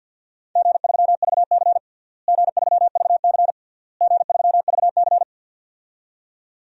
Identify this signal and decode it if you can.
Morse code, heard as a keyed tone, about 51 wpm, 700 Hz. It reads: G3VC G3VC G3VC
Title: DXCC-100 Call Signs - 4 Characters - Repeated 3x - 50wpm